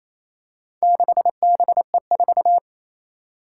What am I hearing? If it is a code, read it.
Morse code, heard as a keyed tone, about 27 wpm, 700 Hz. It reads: 6BE4
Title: Four Challenging Letter-Number Combinations 28wpm